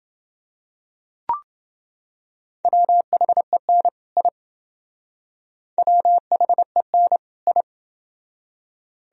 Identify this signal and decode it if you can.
Morse code, heard as a keyed tone, about 28 wpm, 700 Hz. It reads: WHEN I WHEN I